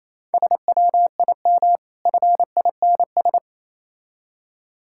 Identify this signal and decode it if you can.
Morse code, heard as a keyed tone, about 27 wpm, 690 Hz. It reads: SWIM FINS
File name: Easy Words - Speed-Racing with Hints - 22wpm